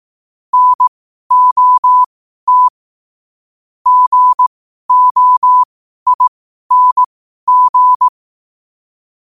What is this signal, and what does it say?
Morse code, heard as a keyed tone, about 18 wpm, spaced as for 12 wpm, 1000 Hz. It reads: NOT GOING
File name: qotdfiles / 12wpm